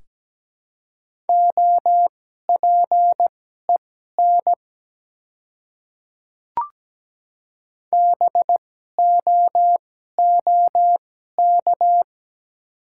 Morse code, heard as a keyed tone, about 17 wpm, spaced as for 12 wpm, 700 Hz. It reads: OPEN BOOK
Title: 2x Character Spacing - Top 300 Words - 17wpm